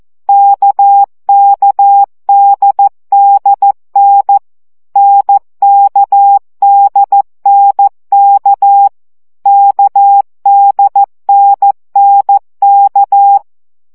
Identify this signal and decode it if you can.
Morse code, heard as a keyed tone, about 14 wpm, 800 Hz. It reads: KKDDN NKDNK KDNNK
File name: CW gyakorlas n d k 12 baud